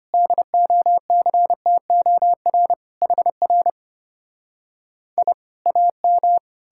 Morse code, heard as a keyed tone, about 33 wpm, 700 Hz. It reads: DOCTOR HR I AM